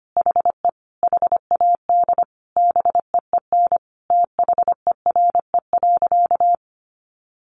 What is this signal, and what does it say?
Morse code, heard as a keyed tone, about 25 wpm, 700 Hz. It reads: HE HAD BEEN THERE.
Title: Sentences from Top 300 Words 25wpm